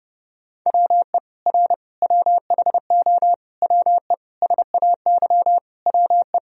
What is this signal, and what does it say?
Morse code, heard as a keyed tone, about 32 wpm, 700 Hz. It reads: WE R WHO WE SAY WE
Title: Sentences from Top 200 Words and Abbreviations 30wpm